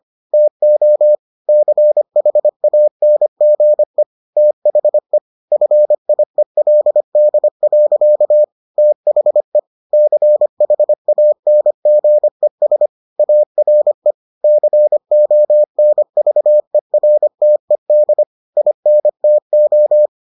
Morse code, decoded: TO CHANGE THE FIELD. THE CHANGES ARE CONVERTED INTO

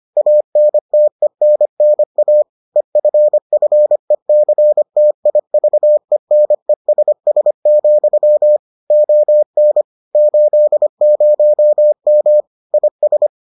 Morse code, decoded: ANTENNA EFFECTIVENESS, ON 80M IS

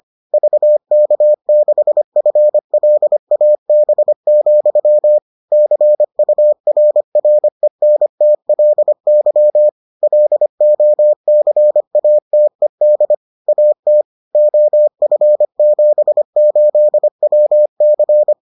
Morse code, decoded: VK6FLAB, CURRENTLY LOCATED AT OF78WC

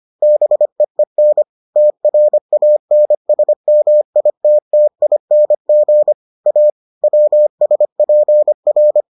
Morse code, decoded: BEEN TRANSMITTING A WSPR

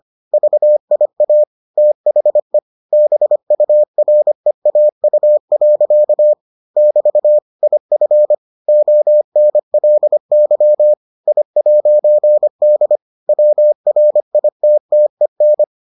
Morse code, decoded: VIA THE BUREAU. <BT> IF ONLY I'D WRITTEN

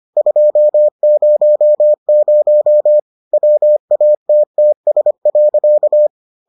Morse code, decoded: 200 WATTS.

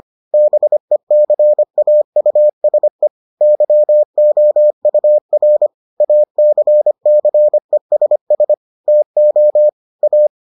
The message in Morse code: BECAUSE YOUR ACCESS TO A

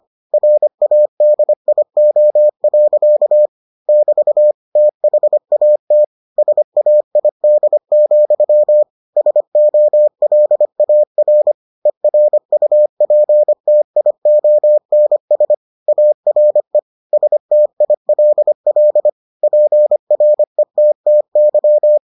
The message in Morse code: RADIO. <BT> THAT SAID, SOLAR ERUPTIONS ARE STILL PRETTY

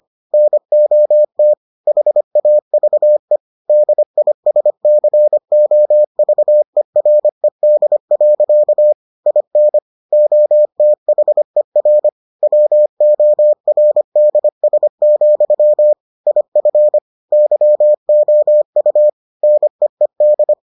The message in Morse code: NOT HAVE DISCOVERED. IN OTHER WORDS, IF YOU NEED

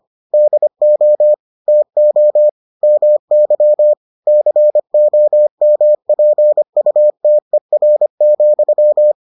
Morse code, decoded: DO TO MY COMPUTER,

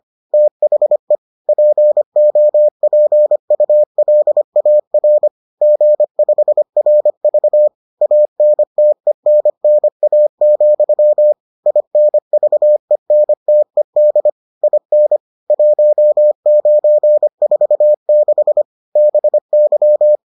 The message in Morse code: THE POPULAR G5RV ANTENNA, INVENTED IN 1946 BY

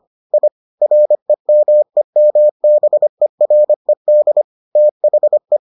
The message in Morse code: I REMEMBERED THE

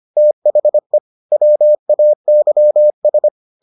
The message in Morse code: THE WAYS

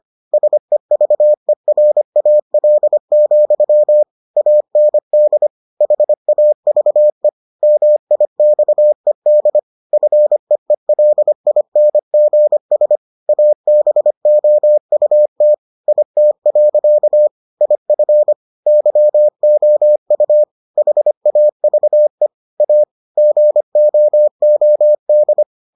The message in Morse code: SEVERAL, AND HAVE MIXED FEELINGS ABOUT IT. IF YOU HAVE A GOOD

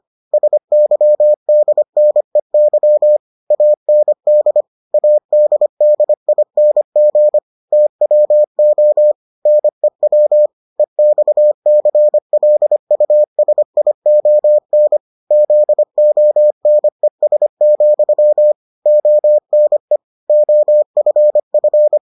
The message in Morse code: SYDNEY AND ADDING TWO NEW EXCLUSION ZONES, ONE OFF

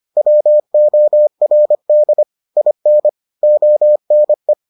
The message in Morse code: WORD IN ONE

